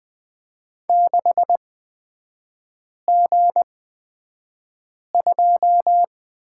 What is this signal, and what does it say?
Morse code, heard as a keyed tone, about 20 wpm, spaced as for 5 wpm, 700 Hz. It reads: 6G2
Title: 250 Lesson 40e - BK - Three Characters 20wpm